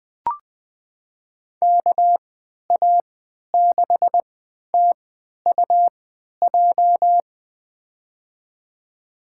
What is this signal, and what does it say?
Morse code, heard as a keyed tone, about 20 wpm, spaced as for 11 wpm, 700 Hz. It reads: KA6TUJ
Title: USA Call Signs - 3x Character Spacing - 20wpm